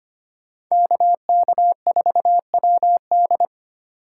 Morse code, heard as a keyed tone, about 25 wpm, 700 Hz. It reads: KK4WD
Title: USA Call Signs - 5 Characters - 3x Character Spacing - 25wpm